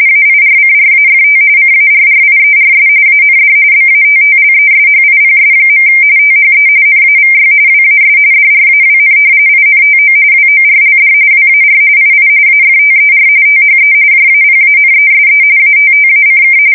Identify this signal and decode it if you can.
RTTY, about 45.45 baud. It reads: AIRCRAFT CARRIERS AND BATTLESHIPS. THE NAVY WAS DRAWN INTO WORLD WAR II AFTER THE JAPANESE ATTACK
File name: fsk170 / usnavy1